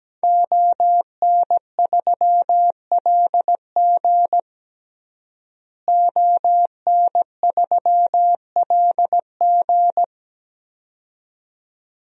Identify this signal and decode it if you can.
Morse code, heard as a keyed tone, about 17 wpm, 700 Hz. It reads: ON3LG ON3LG
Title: DXCC-100 Call Signs - 5 Characters - Repeated 2x - 17wpm